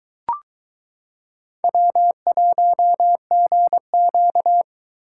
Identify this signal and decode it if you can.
Morse code, heard as a keyed tone, about 22 wpm, 700 Hz. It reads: W1GQ